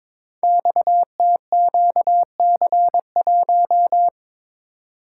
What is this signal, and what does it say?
Morse code, heard as a keyed tone, about 22 wpm, 700 Hz. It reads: XTQC1
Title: Code Groups - 5 Letter:Numbers - 22wpm